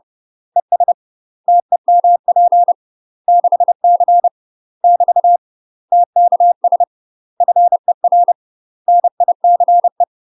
Morse code, heard as a keyed tone, about 30 wpm, 700 Hz. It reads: ES TEMP 6C <BT> TKS FER NICE
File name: UK MASTER QSOs TRAINING / 30WPM / UK MASTER QSOs 0041